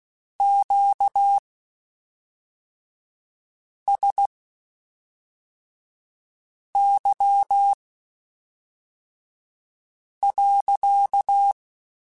Morse code, heard as a keyed tone, about 16 wpm, spaced as for 3 wpm, 800 Hz. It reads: QSY.
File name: Q Calls / Q Calls-mp3s / Q Calls sppc @5